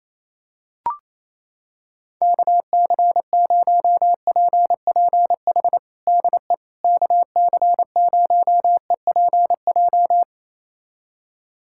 Morse code, heard as a keyed tone, about 28 wpm, 700 Hz. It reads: KC0PPH DE KC0EPJ
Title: USA Callsign DE Callsign 28wpm